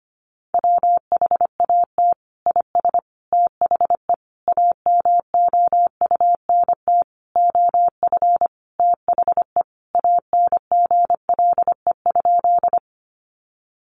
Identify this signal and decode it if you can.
Morse code, heard as a keyed tone, about 25 wpm, 700 Hz. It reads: WHAT IS THE AMOUNT OF THE ANGLE?
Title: Sentences from Top 1500 Words 25wpm